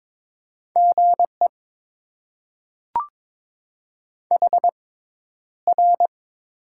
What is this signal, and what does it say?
Morse code, heard as a keyed tone, about 22 wpm, spaced as for 11 wpm, 700 Hz. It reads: GE HR
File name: Top 90 QSO Elements - 6x Character Spacing - 22wpm